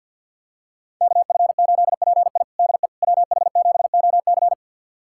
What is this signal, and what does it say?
Morse code, heard as a keyed tone, about 49 wpm, 700 Hz. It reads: KF8PI DE WH7OC
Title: USA Callsign DE Callsign - 5 Characters - 50wpm